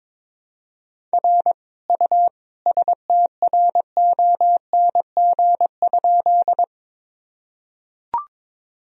Morse code, heard as a keyed tone, about 22 wpm, 700 Hz. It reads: R U STRONG?
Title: Sentences from Top 500 Words and Abbreviations 22wpm